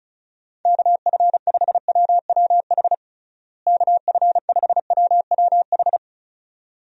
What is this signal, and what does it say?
Morse code, heard as a keyed tone, about 36 wpm, 700 Hz. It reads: KF5WWH KF5WWH
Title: USA Call Signs - 6 Characters - Repeated 2x - 35wpm